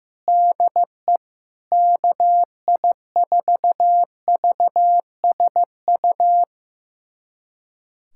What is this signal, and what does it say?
Morse code, heard as a keyed tone, about 15 wpm, 700 Hz. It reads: DE KI4VSU
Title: USA Callsign DE Callsign - 6 Characters - 15wpm